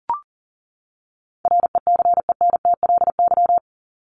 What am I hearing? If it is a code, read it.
Morse code, heard as a keyed tone, about 40 wpm, 700 Hz. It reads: RECENTLY
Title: Top 1001-1250 Words 40wpm